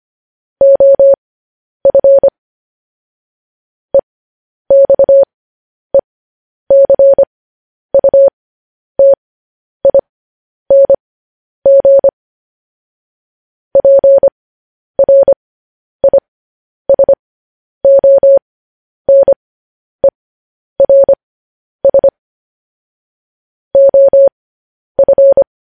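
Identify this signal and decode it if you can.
Morse code, heard as a keyed tone, about 25 wpm, spaced as for 10 wpm, 550 Hz. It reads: OF EXECUTING PRISONERS OF
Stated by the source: Morse code news headlines.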